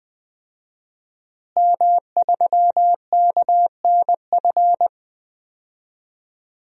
Morse code, heard as a keyed tone, about 20 wpm, 700 Hz. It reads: M3KNF
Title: DXCC-100 Call Signs - 5 Characters - Repeated 2x - 20wpm